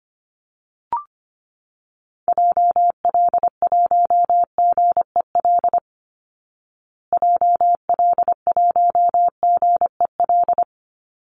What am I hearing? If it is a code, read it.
Morse code, heard as a keyed tone, about 25 wpm, 700 Hz. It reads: JL1GEL JL1GEL
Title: Call Signs Repeated 3x - 25wpm